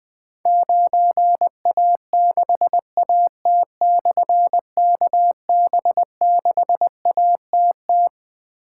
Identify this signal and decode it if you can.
Morse code, heard as a keyed tone, about 21 wpm, 700 Hz. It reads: 9A6AT/KB6ATT
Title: Difficult Call Signs 20wpm